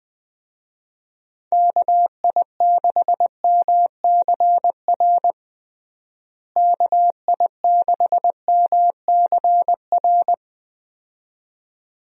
Morse code, heard as a keyed tone, about 20 wpm, 700 Hz. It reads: KI6MCR KI6MCR
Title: USA Call Signs - 6 Characters - Repeated 2x - 20wpm